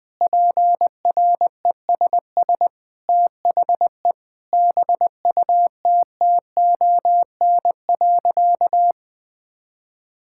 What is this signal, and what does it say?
Morse code, heard as a keyed tone, about 21 wpm, 700 Hz. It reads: PRESS THE BUTTON.
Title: Sentences with 6-Word Phrases - 20wpm